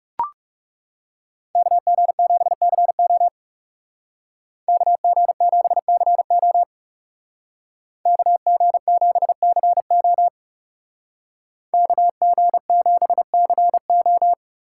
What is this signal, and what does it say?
Morse code, heard as a keyed tone, about 39 wpm, 700 Hz. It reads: KG7CO KG7CO KG7CO KG7CO